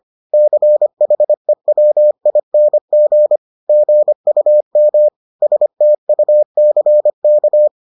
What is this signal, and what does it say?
Morse code, heard as a keyed tone, about 25 wpm, 600 Hz. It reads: CHEWING GUM STUCK